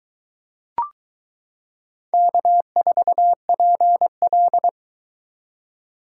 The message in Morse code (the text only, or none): K4PL